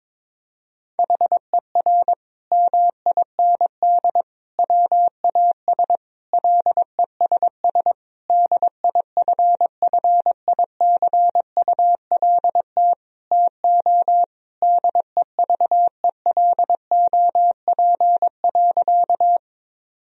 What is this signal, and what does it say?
Morse code, heard as a keyed tone, about 22 wpm, 700 Hz. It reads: HER MIND WAS LESS DIFFICULT TO DEVELOP.
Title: Sentences from Top 1000 Words - Review Part 2 - 22wpm